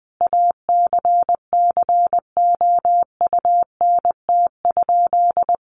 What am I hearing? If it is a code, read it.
Morse code, heard as a keyed tone, about 20 wpm, 700 Hz. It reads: ACCOUNT?
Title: Sentences from Top 1500 Words 20wpm